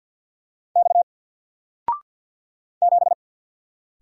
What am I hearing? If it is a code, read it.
Morse code, heard as a keyed tone, about 49 wpm, 700 Hz. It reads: X 7
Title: Single Letter-Number 50wpm